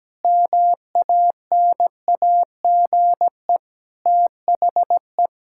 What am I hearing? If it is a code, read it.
Morse code, heard as a keyed tone, about 17 wpm, 700 Hz. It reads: MANAGE THE
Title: Sentences from Top 1600 Words 17wpm